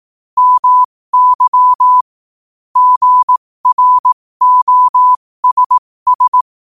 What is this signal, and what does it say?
Morse code, heard as a keyed tone, about 18 wpm, spaced as for 15 wpm, 1000 Hz. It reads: MY GROSS
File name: qotdfiles / 15wpm